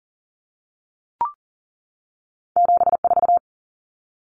Morse code, heard as a keyed tone, about 40 wpm, 700 Hz. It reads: 74
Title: Common Cut Numbers - 2-Digit 40wpm